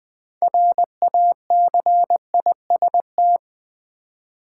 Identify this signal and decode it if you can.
Morse code, heard as a keyed tone, about 20 wpm, 690 Hz. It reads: RACIST
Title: Build A Word - 2 Letters At A Time - 6-Letter Words - 20wpm